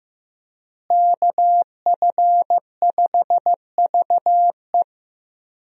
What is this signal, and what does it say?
Morse code, heard as a keyed tone, about 15 wpm, 700 Hz. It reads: KF5VE
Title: USA Call Signs - 5 Characters - Repeated 2x - 15wpm